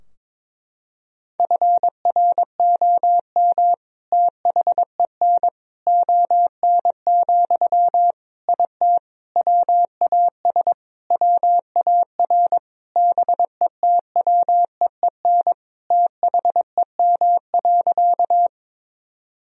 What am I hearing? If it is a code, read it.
Morse code, heard as a keyed tone, about 22 wpm, 700 Hz. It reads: FROM THEN ON, IT WAS WAR BETWEEN THEM.
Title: Sentences from Top 1000 Words - Review Part 2 - 22wpm